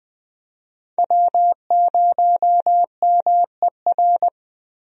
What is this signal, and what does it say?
Morse code, heard as a keyed tone, about 20 wpm, 700 Hz. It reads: W0MER
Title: USA Call Signs - 5 Characters - Repeated 3x - 20wpm